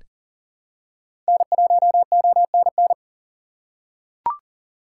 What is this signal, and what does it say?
Morse code, heard as a keyed tone, about 40 wpm, 700 Hz. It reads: N1ONN
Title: USA Call Signs - 5 Characters - 4x Character Spacing - 40wpm